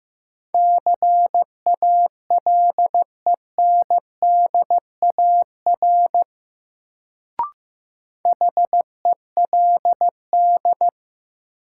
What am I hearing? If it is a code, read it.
Morse code, heard as a keyed tone, about 16 wpm, 700 Hz. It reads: CALENDAR HELD
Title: Sentences from Top 1600 Words - New Word Drill - 15wpm